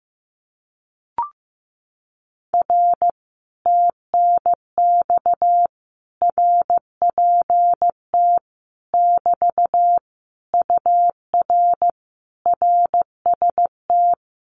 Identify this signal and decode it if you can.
Morse code, heard as a keyed tone, about 15 wpm, 700 Hz. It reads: R TNX RPT <BT> UR RST
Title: Common QSO Phrases Encore 15wpm